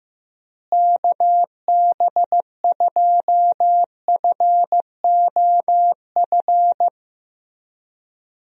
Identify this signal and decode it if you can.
Morse code, heard as a keyed tone, about 15 wpm, 700 Hz. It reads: KB2FOF
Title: USA Call Signs - 6 Characters - Repeated 2x - 15wpm